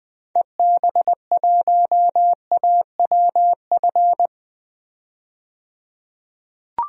Morse code, heard as a keyed tone, about 20 wpm, 700 Hz. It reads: EB1AWF
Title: DXCC-100 Call Signs - 6 Characters - 20wpm